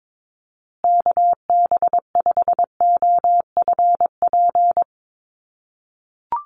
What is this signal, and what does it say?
Morse code, heard as a keyed tone, about 22 wpm, 700 Hz. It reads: KB5OFP
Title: USA Call Signs Repeated 2x 22wpm